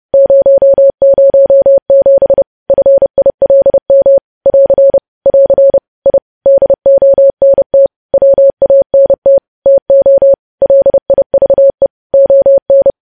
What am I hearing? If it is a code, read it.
Morse code, heard as a keyed tone, about 30 wpm, 550 Hz. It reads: 007 FILM <AR> <AR> I DONT WANT TO LIVE ON
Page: News Headlines in Morse Code at 30 WPM